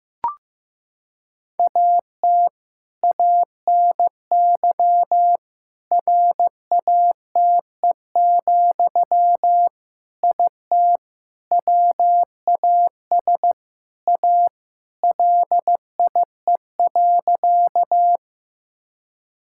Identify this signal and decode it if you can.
Morse code, heard as a keyed tone, about 15 wpm, 700 Hz. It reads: AT ANY RATE, IT WAS A LIE.
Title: Sentences from Top 1600 Words 15wpm